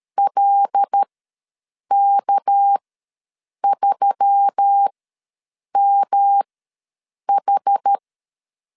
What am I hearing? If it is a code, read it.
Morse code, heard as a keyed tone, about 13 wpm, spaced as for 7 wpm, 800 Hz. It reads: LK3MH
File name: LEZIONI / 6 PKX0 / 6 PK ws 12wpm 3116